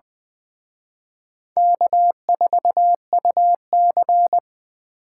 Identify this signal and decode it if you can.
Morse code, heard as a keyed tone, about 24 wpm, 700 Hz. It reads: K4UC
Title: Speed Racing - USA 1x2 Call Signs 20wpm